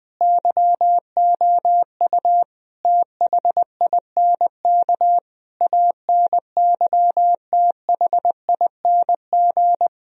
Morse code, heard as a keyed tone, about 20 wpm, 700 Hz. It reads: YOU THINK ANYTHING
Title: Sentences from Top 1000 Words - Review Part 2 - 20wpm